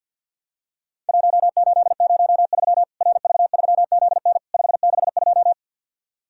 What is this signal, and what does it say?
Morse code, heard as a keyed tone, about 50 wpm, 700 Hz. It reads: 1803 RV3ZN 562
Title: Worked All Europe DX Contest - QTC Exchange with Repeats - 50wpm